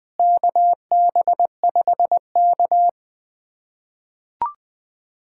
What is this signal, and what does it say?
Morse code, heard as a keyed tone, about 20 wpm, 700 Hz. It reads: KB5K
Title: USA Callsign DE Callsign - 4 Characters - 20wpm